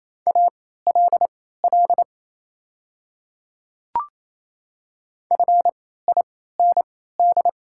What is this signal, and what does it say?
Morse code, heard as a keyed tone, about 28 wpm, spaced as for 16 wpm, 700 Hz. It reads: ALL FIND
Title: 3x Character Spacing - Top 100 Words - 28wpm